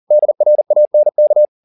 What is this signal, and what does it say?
Morse code, heard as a keyed tone, about 40 wpm, 600 Hz. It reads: DRANK